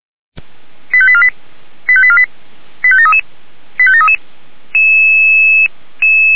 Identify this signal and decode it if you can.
Five-tone selective call (ZVEI), heard as quick runs of five tones, tones of about 70 ms. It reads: 85848 85848 85733 85733
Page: SDR/ZVEI